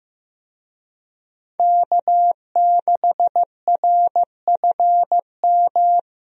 Morse code, heard as a keyed tone, about 15 wpm, 700 Hz. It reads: K6RFM
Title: USA Call Signs - 5 Characters - Repeated 2x - 15wpm